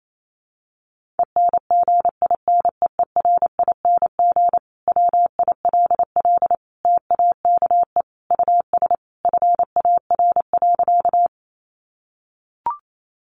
Morse code, heard as a keyed tone, about 28 wpm, 700 Hz. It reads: ENGINEERING WILL TAKE US FAR.
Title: Sentences from Top 1500 Words 28wpm